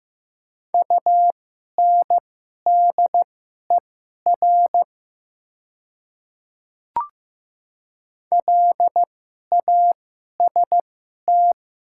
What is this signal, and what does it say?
Morse code, heard as a keyed tone, about 15 wpm, spaced as for 11 wpm, 700 Hz. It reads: UNDER LAST
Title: 2x Character Spacing - Top 300 Words - 15wpm